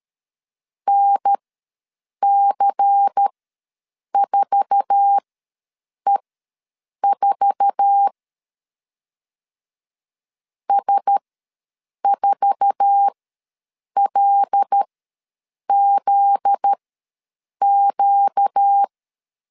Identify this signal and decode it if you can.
Morse code, heard as a keyed tone, about 13 wpm, spaced as for 7 wpm, 800 Hz. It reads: NC4E4 S4LZQ
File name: LEZIONI / 8 YJ45 / 8 YJ4 ws 12wpm 3249